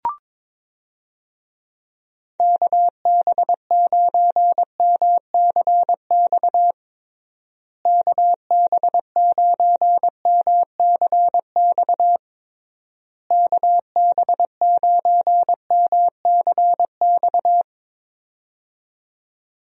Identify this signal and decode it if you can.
Morse code, heard as a keyed tone, about 22 wpm, 700 Hz. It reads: KB9MCX KB9MCX KB9MCX
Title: USA Call Signs - 6 Characters - Repeated 3x - 22wpm